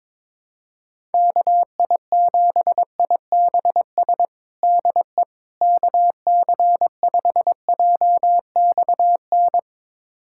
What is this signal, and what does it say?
Morse code, heard as a keyed tone, about 22 wpm, 700 Hz. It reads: KI7IBS DE KC5JXN
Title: USA Callsign DE Callsign 22wpm